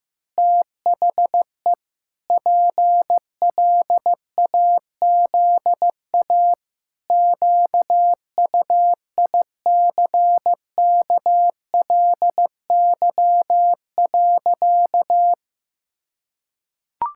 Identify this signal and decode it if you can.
Morse code, heard as a keyed tone, about 16 wpm, 700 Hz. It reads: THE PLAZA QUICKLY.
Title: Pangram Challenge v2 - 15wpm